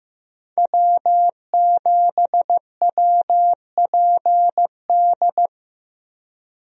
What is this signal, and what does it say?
Morse code, heard as a keyed tone, about 15 wpm, 700 Hz. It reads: W7WPD
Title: USA Call Signs - 5 Characters - Repeated 2x - 15wpm